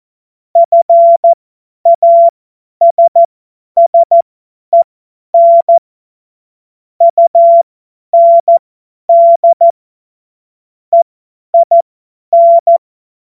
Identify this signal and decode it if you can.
Morse code, heard as a keyed tone, about 14 wpm, spaced as for 10 wpm, 680 Hz. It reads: FASSEN UND EIN
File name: Morsekurs / Witze / tz 08. November 2022 50 F70